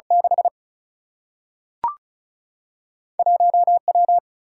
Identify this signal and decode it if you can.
Morse code, heard as a keyed tone, about 36 wpm, 700 Hz. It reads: 6 1 W
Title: Single-Letter Single-Number 35wpm